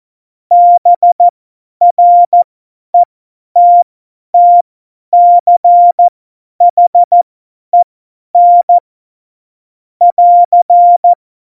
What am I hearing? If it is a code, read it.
Morse code, heard as a keyed tone, about 14 wpm, spaced as for 10 wpm, 700 Hz. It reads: BRETTCHEN <AR>
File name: Morsekurs / Witze / Witz 19. Juni 2018, 50 F70